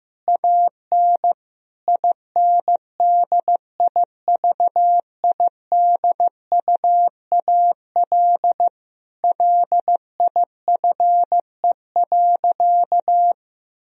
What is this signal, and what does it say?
Morse code, heard as a keyed tone, about 15 wpm, 700 Hz. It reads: AN INDIVIDUAL LIFE.
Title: Sentences from Top 1000 Words - Review Part 4 - 15wpm